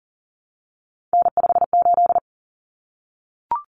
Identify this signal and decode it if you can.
Morse code, heard as a keyed tone, about 40 wpm, 700 Hz. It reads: N58
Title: Common Cut Numbers - 3-Digit 40wpm